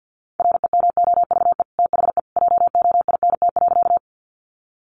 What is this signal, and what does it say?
Morse code, heard as a keyed tone, about 50 wpm, 700 Hz. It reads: REMOVE THE JOINT.
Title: Sentences from Top 1300 Words 50wpm